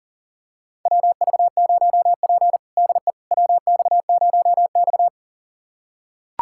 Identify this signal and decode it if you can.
Morse code, heard as a keyed tone, about 40 wpm, 700 Hz. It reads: WV0P DE WX0X
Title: USA Callsign DE Callsign - 4 Characters - 40wpm